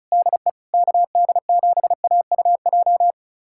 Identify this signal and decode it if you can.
Morse code, heard as a keyed tone, about 36 wpm, 700 Hz. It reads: DE KD7AUJ
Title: USA Callsign DE Callsign - 6 Characters - 35wpm